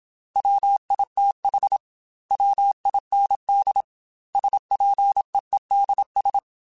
Morse code, heard as a keyed tone, about 27 wpm, 800 Hz. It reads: WITH WIND SPEEDS
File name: Weather Sets / Weather Sets-mp3s / Weather 4 14 @25